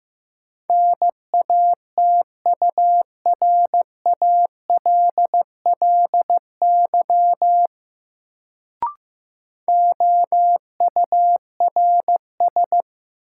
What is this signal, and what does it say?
Morse code, heard as a keyed tone, about 15 wpm, 700 Hz. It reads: NATURALLY OURS
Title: Sentences from Top 1300 Words - New Word Drill Alt - 15wpm